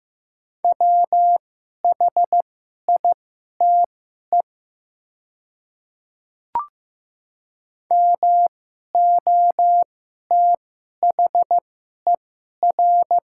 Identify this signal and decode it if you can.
Morse code, heard as a keyed tone, about 15 wpm, spaced as for 11 wpm, 700 Hz. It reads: WHITE MOTHER
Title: 2x Character Spacing - Top 300 Words - 15wpm